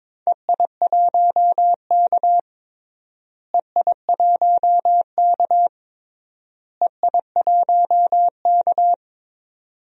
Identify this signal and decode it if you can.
Morse code, heard as a keyed tone, about 21 wpm, 700 Hz. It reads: EI1K EI1K EI1K
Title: DXCC-100 Call Signs - 4 Characters - Repeated 3x - 22wpm